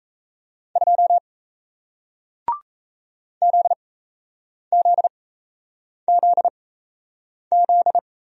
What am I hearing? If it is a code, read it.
Morse code, heard as a keyed tone, about 38 wpm, spaced as for 8 wpm, 700 Hz. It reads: 2 ZZZZ